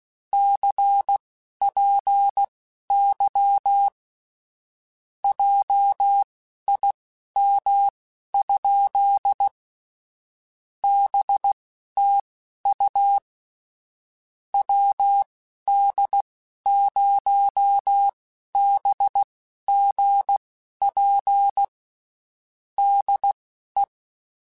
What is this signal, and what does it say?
Morse code, heard as a keyed tone, about 16 wpm, spaced as for 11 wpm, 800 Hz. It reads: CPY JIM? BTU WD0BGP DE